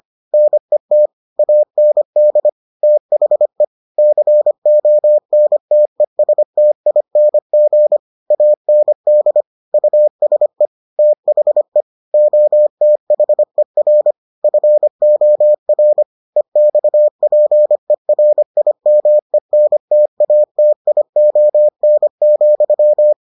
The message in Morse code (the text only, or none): NET AND THE CONTESTING AND USE THE OTHER FOR EXPERIMENTATION,